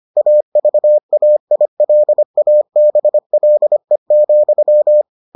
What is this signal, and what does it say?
Morse code, heard as a keyed tone, about 25 wpm, 600 Hz. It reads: AVAILABLE,